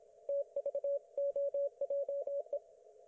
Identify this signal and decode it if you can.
Morse code, heard as a keyed tone, about 26 wpm, 560 Hz. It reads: TVOJE
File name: Archiv / CUC194 / ok1fhi cul194